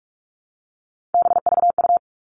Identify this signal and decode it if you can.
Morse code, heard as a keyed tone, about 44 wpm, 700 Hz. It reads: BVU
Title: All Cut Numbers - 3-Digit 45wpm